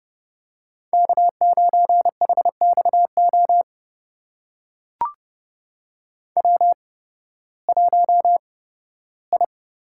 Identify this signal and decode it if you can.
Morse code, heard as a keyed tone, about 31 wpm, spaced as for 16 wpm, 700 Hz. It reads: K9HXO W1I
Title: USA Call Signs - 5 Characters - 8x Character Spacing - 30wpm